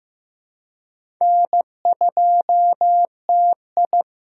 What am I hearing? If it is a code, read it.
Morse code, heard as a keyed tone, about 15 wpm, 700 Hz. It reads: N2TI
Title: USA Call Signs - 4 Characters - Repeated 2x - 15wpm